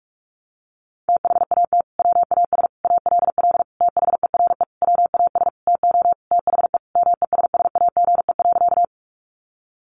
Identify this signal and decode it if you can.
Morse code, heard as a keyed tone, about 44 wpm, 700 Hz. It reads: THAT WAS ALL THERE WAS TO THE MESSAGE.
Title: Sentences from Top 1100 Words 45wpm